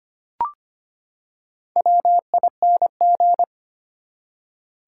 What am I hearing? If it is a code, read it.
Morse code, heard as a keyed tone, about 25 wpm, 700 Hz. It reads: WING
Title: Four Letter Words - 25wpm